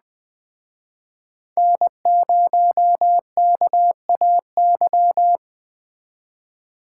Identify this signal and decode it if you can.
Morse code, heard as a keyed tone, about 20 wpm, 700 Hz. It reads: N0KAY
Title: Speed Racing - USA 1x3 Call Signs 17wpm